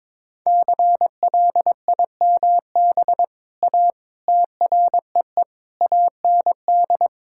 Morse code, heard as a keyed tone, about 21 wpm, 700 Hz. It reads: CLIMB A TREE AND
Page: morse-code-ninja-practice-22wpm